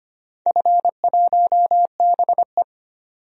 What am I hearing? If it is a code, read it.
Morse code, heard as a keyed tone, about 25 wpm, 700 Hz. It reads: F1BE
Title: DXCC-100 Call Signs - 4 Characters - 25wpm